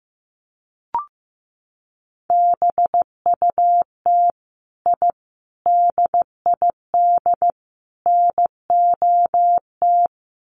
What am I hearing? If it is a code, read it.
Morse code, heard as a keyed tone, about 15 wpm, 700 Hz. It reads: BUT I DID NOT
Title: Sets of 4 Words 15wpm